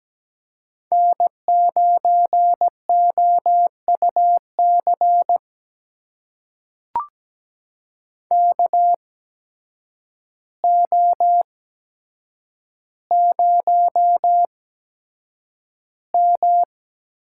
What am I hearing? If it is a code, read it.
Morse code, heard as a keyed tone, about 17 wpm, spaced as for 8 wpm, 700 Hz. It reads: N9OUC KO0M
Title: USA Call Signs - 8x Character Spacing - 17wpm